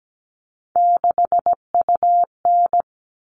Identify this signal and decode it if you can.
Morse code, heard as a keyed tone, about 17 wpm, 700 Hz. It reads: 6UN
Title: All Cut Numbers - 3-Digit 17wpm